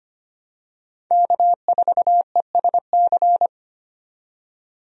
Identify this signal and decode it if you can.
Morse code, heard as a keyed tone, about 25 wpm, 700 Hz. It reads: K4ESC
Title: USA Call Signs - 5 Characters - 25wpm